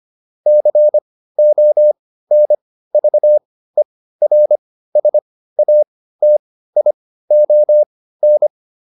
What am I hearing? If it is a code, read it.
Morse code, heard as a keyed tone, about 25 wpm, spaced as for 15 wpm, 600 Hz. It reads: CONVERSATION